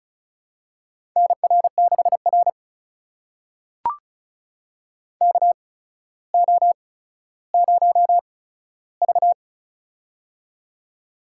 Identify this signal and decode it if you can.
Morse code, heard as a keyed tone, about 36 wpm, spaced as for 15 wpm, 700 Hz. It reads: NR6R KO0V
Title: USA Call Signs - 4 Characters - 8x Character Spacing - 35wpm